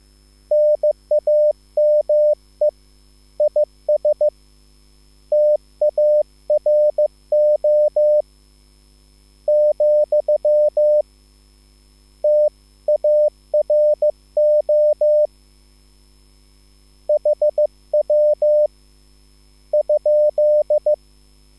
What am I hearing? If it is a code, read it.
Morse code, heard as a keyed tone, about 15 wpm, 600 Hz. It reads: NAME IS TARO , TARO HW ?